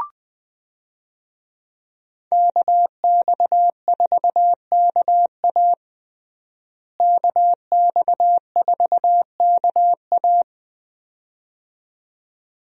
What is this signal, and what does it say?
Morse code, heard as a keyed tone, about 21 wpm, 700 Hz. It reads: KX4KA KX4KA
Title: USA Call Signs - 5 Characters - Repeated 2x - 20wpm